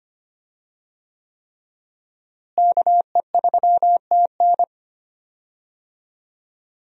Morse code, heard as a keyed tone, about 25 wpm, 700 Hz. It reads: KE3TN
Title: USA Call Signs - 5 Characters - Repeated 3x - 25wpm